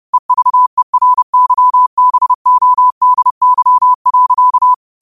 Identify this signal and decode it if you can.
Morse code, heard as a keyed tone, about 30 wpm, 1000 Hz. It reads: EVERYBODY.
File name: qotdfiles / 30wpm